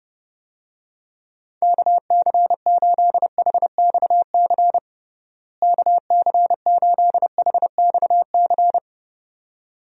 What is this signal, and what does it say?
Morse code, heard as a keyed tone, about 29 wpm, 700 Hz. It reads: KC8HXC KC8HXC
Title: USA Call Signs - 6 Characters - Repeated 2x - 30wpm